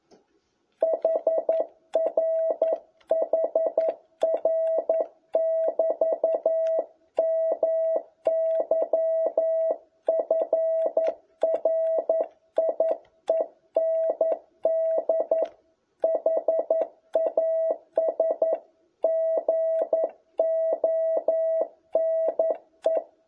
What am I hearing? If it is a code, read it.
Morse code, heard as a keyed tone, about 11 wpm, 650 Hz. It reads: HR HR<BT>MY FRIEND HAS GONE
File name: jh8cbh / 2021cw